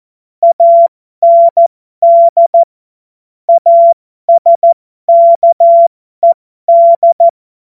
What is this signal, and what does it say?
Morse code, heard as a keyed tone, about 14 wpm, spaced as for 12 wpm, 680 Hz. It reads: AND ASKED
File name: Morsekurs / Witze